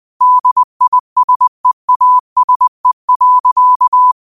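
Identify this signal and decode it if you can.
Morse code, heard as a keyed tone, about 20 wpm, 1000 Hz. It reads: DISEASE.
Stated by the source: Morse code quote of the day at 20 wpm.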